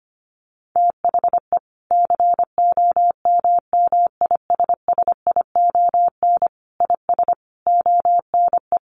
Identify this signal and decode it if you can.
Morse code, heard as a keyed tone, about 25 wpm, 700 Hz. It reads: THE COMMISSION IS ONE
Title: Sentences from Top 1400 Words 25wpm